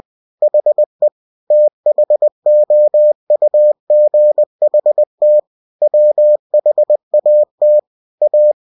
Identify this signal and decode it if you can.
Morse code, heard as a keyed tone, about 20 wpm, 600 Hz. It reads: HE THOUGHT WHAT A
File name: cw-books / winnie-the-pooh / winnie-20wpm-page-0099